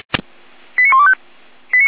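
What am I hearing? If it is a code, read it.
Five-tone selective call (ZVEI), heard as quick runs of five tones, tones of about 80 ms. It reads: 89125
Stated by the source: ZVEI five-tone selective call received on SDR.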